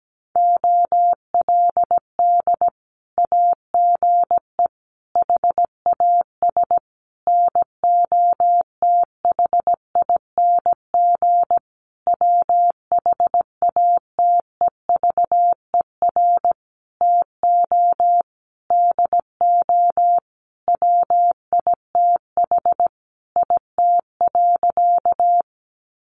Morse code, heard as a keyed tone, about 17 wpm, 700 Hz. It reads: OLD AGE HAS NOTHING WHATEVER TO DO WITH IT.
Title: Sentences from Top 1100 Words 17wpm